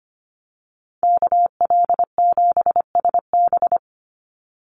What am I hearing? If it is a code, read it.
Morse code, heard as a keyed tone, about 25 wpm, 700 Hz. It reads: KL7SB
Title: Call Signs - 25wpm